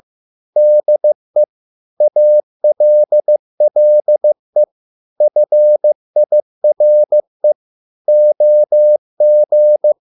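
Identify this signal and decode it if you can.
Morse code, heard as a keyed tone, about 15 wpm, 600 Hz. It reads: DE ALLE FIRE OG
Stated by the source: Morse code at a slow 15 wpm from CW med Gnister.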